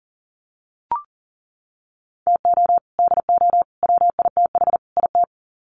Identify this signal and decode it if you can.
Morse code, heard as a keyed tone, about 40 wpm, 700 Hz. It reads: TO DO WITH IT
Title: Sets of 4 Words 40wpm